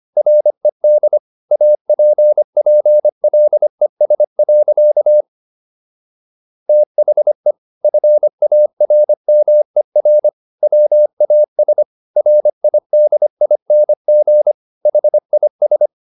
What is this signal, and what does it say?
Morse code, heard as a keyed tone, about 25 wpm, 600 Hz. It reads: RED APPLES. THE FARMER WAS RIDING HIS